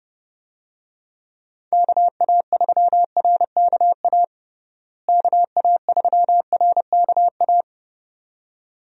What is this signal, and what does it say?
Morse code, heard as a keyed tone, about 31 wpm, 700 Hz. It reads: KA3RKA KA3RKA
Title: USA Call Signs - 6 Characters - Repeated 2x - 30wpm